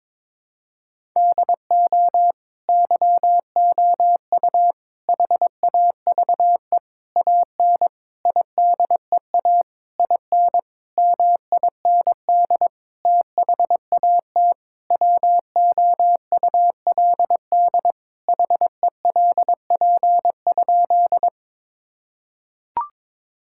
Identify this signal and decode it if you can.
Morse code, heard as a keyed tone, about 22 wpm, 700 Hz. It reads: DO YOU HAVE AN IDEA IN MIND THAT WOULD HELP?
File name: Sentences with 9-Word Phrases - 22wpm